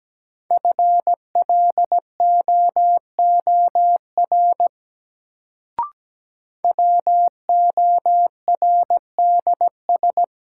Morse code, heard as a keyed tone, about 17 wpm, 700 Hz. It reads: FLOOR WORDS
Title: Sets of 3 Words - New Word Drill - 17wpm